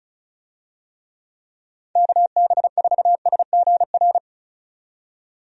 Morse code, heard as a keyed tone, about 36 wpm, 700 Hz. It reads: KB4SGR
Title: USA Call Signs - 6 Characters - 35wpm